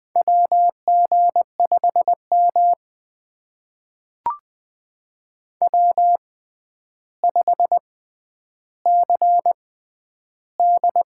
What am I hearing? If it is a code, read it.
Morse code, heard as a keyed tone, about 20 wpm, spaced as for 10 wpm, 700 Hz. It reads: WG5M W5CD
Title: USA Call Signs - 6x Character Spacing - 20wpm